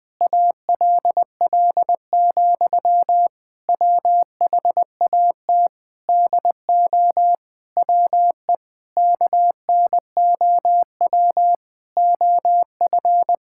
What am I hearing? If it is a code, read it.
Morse code, heard as a keyed tone, about 20 wpm, 700 Hz. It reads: ALL, WHAT DO WE KNOW OF
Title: Sentences from Top 1600 Words 20wpm